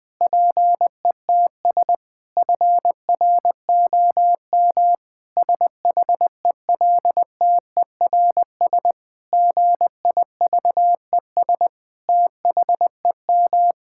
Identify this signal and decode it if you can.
Morse code, heard as a keyed tone, about 20 wpm, 700 Hz. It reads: PETS FROM SHELTERS GIVES THEM
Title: Difficult Sentences with Hints - 20wpm